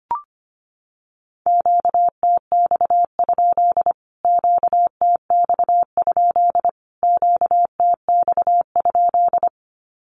Morse code, heard as a keyed tone, about 25 wpm, 700 Hz. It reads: QTX? QTX? QTX?
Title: Q-Codes Repeated 3x 25wpm